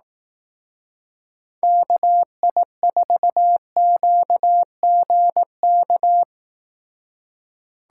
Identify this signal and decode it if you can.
Morse code, heard as a keyed tone, about 18 wpm, 700 Hz. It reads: KI4QGK
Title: Speed Racing - USA 2x3 Call Signs 15wpm